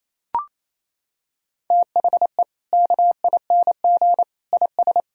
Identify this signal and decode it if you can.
Morse code, heard as a keyed tone, about 27 wpm, 700 Hz. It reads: THE KING IS
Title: Sentences from Top 1000 Words - Review Part 1 - 28wpm